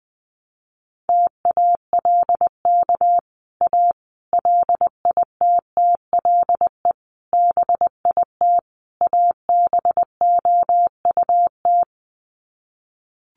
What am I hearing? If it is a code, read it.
Morse code, heard as a keyed tone, about 20 wpm, 700 Hz. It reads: TALK A LITTLE BIT ABOUT
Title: Sets of 5 Words Encore 20wpm